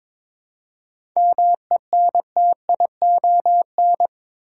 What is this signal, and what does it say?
Morse code, heard as a keyed tone, about 22 wpm, 700 Hz. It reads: MENTION
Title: Seven Letter Words - 22wpm